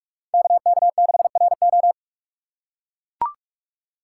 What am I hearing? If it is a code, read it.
Morse code, heard as a keyed tone, about 41 wpm, 700 Hz. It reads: KK6RO
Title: USA Call Signs - 5 Characters - 6x Character Spacing - 45wpm